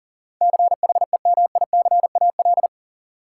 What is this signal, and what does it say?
Morse code, heard as a keyed tone, about 39 wpm, 700 Hz. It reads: CHEMICAL
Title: Sentences from Top 1200 Words - New Word Drill - 40wpm